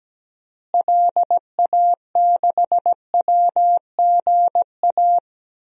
Morse code, heard as a keyed tone, about 17 wpm, 700 Hz. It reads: LA6WGA
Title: DXCC-100 Call Signs - 6 Characters - Repeated 2x - 17wpm